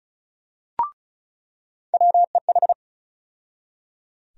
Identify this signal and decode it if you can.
Morse code, heard as a keyed tone, about 34 wpm, 700 Hz. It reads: WEH
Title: USA Call Sign Postfixes - 3 Letters 35wpm